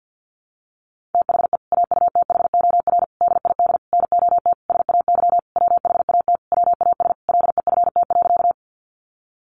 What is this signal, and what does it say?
Morse code, heard as a keyed tone, about 50 wpm, 700 Hz. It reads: THE AUTHOR DID NOT SAY WHAT WAS LEFT.
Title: Sentences from Top 1200 Words 50wpm